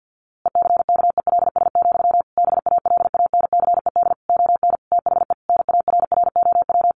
Morse code, heard as a keyed tone, about 50 wpm, 700 Hz. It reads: ECKELS, BALANCED ON THE NARROW
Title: A Sound Of Thunder 50wpm